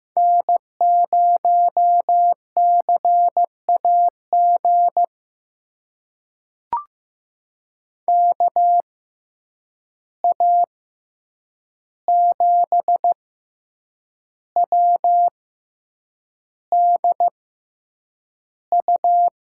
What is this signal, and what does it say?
Morse code, heard as a keyed tone, about 15 wpm, spaced as for 7 wpm, 700 Hz. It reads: N0CAG KA7WDU
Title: USA Call Signs - 6x Character Spacing - 15wpm